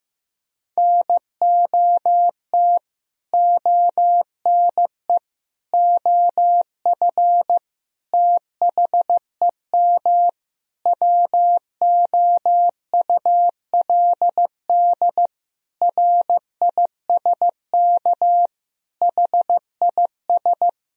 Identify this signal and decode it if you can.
Morse code, heard as a keyed tone, about 15 wpm, 700 Hz. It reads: NOT ONE OF THEM WOULD RISK HIS
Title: Sentences from Top 1000 Words - Review Part 4 - 15wpm